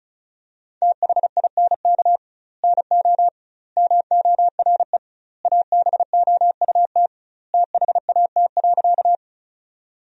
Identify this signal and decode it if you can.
Morse code, heard as a keyed tone, about 36 wpm, 700 Hz. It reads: THINK NO MORE ABOUT THAT.
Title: 2x Word-Spacing - Sentences from Top 200 Words - 35wpm